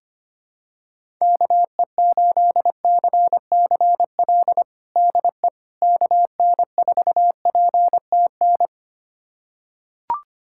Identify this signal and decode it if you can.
Morse code, heard as a keyed tone, about 25 wpm, 700 Hz. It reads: KE8CCL DE KN4PTN
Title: USA Callsign DE Callsign - 6 Characters - 25wpm